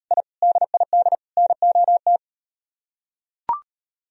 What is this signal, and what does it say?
Morse code, heard as a keyed tone, about 38 wpm, 700 Hz. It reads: I DID NOT